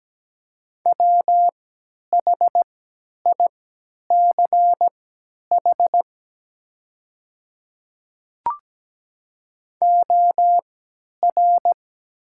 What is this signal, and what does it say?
Morse code, heard as a keyed tone, about 17 wpm, spaced as for 10 wpm, 700 Hz. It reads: WHICH OR
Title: 3x Character Spacing - Top 200 Words - 17wpm